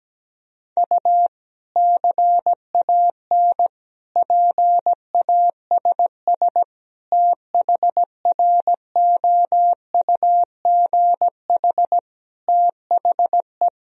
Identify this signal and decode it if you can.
Morse code, heard as a keyed tone, about 17 wpm, 700 Hz. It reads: U CAN PASS THROUGH THE
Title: Sentences from Top 400 Words and Abbreviations 17wpm